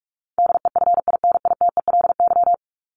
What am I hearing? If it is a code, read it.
Morse code, heard as a keyed tone, about 44 wpm, 700 Hz. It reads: DEFINITELY
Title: Top 1751-2000 Words 45wpm